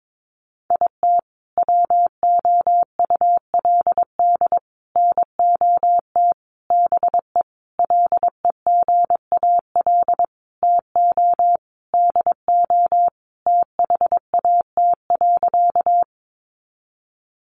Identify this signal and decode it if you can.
Morse code, heard as a keyed tone, about 22 wpm, 700 Hz. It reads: IT WOULD NOT BE LEGAL TO DO THAT.
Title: Sentences from Top 1200 Words 22wpm